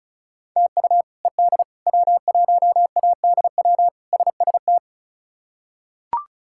Morse code, heard as a keyed tone, about 34 wpm, 700 Hz. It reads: TU ED W1ADW SST
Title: K1USN SST QSO Practice - 35wpm